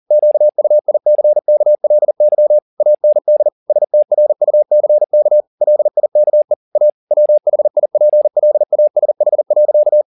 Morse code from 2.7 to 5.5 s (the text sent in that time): AND STRUCK